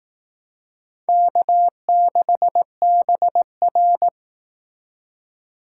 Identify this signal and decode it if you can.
Morse code, heard as a keyed tone, about 18 wpm, 700 Hz. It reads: K6BR
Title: Speed Racing - USA 1x2 Call Signs 15wpm